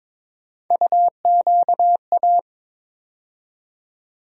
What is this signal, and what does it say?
Morse code, heard as a keyed tone, about 22 wpm, 700 Hz. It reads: UQA
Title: USA Call Sign Postfixes - 3 Letters 22wpm